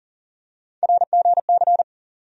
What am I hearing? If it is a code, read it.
Morse code, heard as a keyed tone, about 40 wpm, 700 Hz. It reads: RGC
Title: USA Call Sign Postfixes - 3 Letters 40wpm